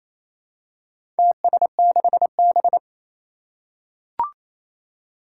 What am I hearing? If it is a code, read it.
Morse code, heard as a keyed tone, about 27 wpm, 700 Hz. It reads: TS6B
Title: Four Challenging Letter-Number Combinations 28wpm